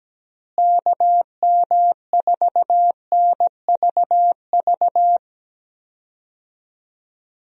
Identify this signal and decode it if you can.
Morse code, heard as a keyed tone, about 17 wpm, 700 Hz. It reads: KM4NVV
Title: USA Call Signs - 6 Characters - Repeated 2x - 17wpm